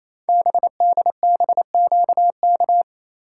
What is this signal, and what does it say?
Morse code, heard as a keyed tone, about 27 wpm, 700 Hz. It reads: BDBQK
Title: Code Groups - 5 Letters - 28wpm